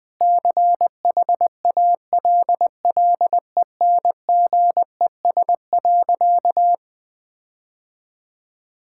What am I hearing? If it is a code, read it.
Morse code, heard as a keyed tone, about 20 wpm, 700 Hz. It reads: CHALLENGES.
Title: Sentences with 6-Word Phrases - 20wpm